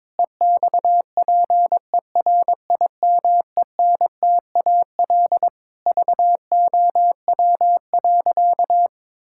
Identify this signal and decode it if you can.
Morse code, heard as a keyed tone, about 22 wpm, 700 Hz. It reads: EXPERIMENTAL VOW.
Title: Pangram Challenge v2 - 22wpm